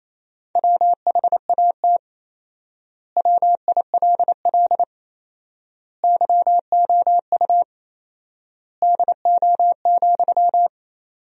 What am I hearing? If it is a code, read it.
Morse code, heard as a keyed tone, about 28 wpm, 700 Hz. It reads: WHAT WILL YOU DO,
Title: 4x Word-Spacing - Sentences from Top 100 Words - 28wpm